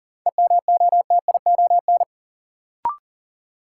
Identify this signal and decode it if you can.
Morse code, heard as a keyed tone, about 40 wpm, 700 Hz. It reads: EMOTION
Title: Sentences from Top 1600 Words - New Word Drill - 40wpm